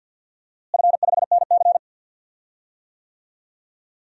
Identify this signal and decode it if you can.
Morse code, heard as a keyed tone, about 50 wpm, 700 Hz. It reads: F5NC
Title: DXCC-100 Call Signs - 4 Characters - Repeated 3x - 50wpm